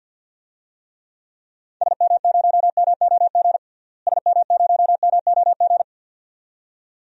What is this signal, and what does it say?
Morse code, heard as a keyed tone, about 48 wpm, 700 Hz. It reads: SM0MOG SM0MOG
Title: DXCC-100 Call Signs - 6 Characters - Repeated 2x - 50wpm